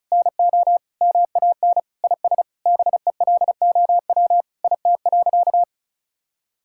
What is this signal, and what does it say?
Morse code, heard as a keyed tone, about 36 wpm, 700 Hz. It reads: NO MAN IS BELOW IT.
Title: Sentences from Top 1000 Words - Review Part 5 - 35wpm